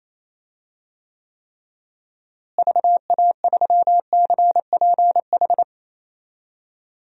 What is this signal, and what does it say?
Morse code, heard as a keyed tone, about 27 wpm, 700 Hz. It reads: VA3CPH
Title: DXCC-100 Call Signs - 6 Characters - Repeated 3x - 28wpm